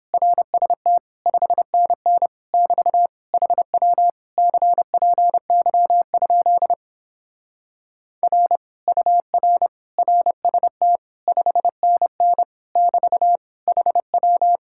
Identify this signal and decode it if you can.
Morse code, heard as a keyed tone, about 29 wpm, 700 Hz. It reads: RST 5NN <BT> HW CPY? R UR RST 5NN <BT> HW